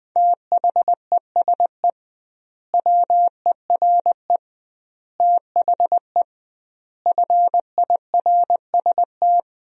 Morse code, heard as a keyed tone, about 20 wpm, 700 Hz. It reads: THESE WERE THE FIRST
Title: 2x Word-Spacing - Sentences from Top 300 Words - 20wpm